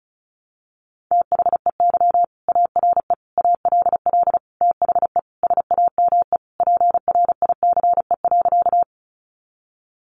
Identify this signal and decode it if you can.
Morse code, heard as a keyed tone, about 34 wpm, 700 Hz. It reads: THEY ARE ALL THE SAME PRICE.
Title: Sentences from Top 600 Words 35wpm